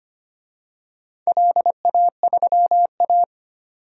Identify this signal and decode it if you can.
Morse code, heard as a keyed tone, about 25 wpm, 690 Hz. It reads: LA3A
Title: DXCC-100 Call Signs - 4 Characters - 25wpm